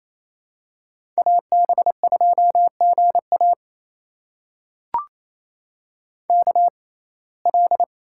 Morse code, heard as a keyed tone, about 28 wpm, spaced as for 20 wpm, 700 Hz. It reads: AB2GA KL
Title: USA Call Signs - 5 Characters - 6x Character Spacing - 28wpm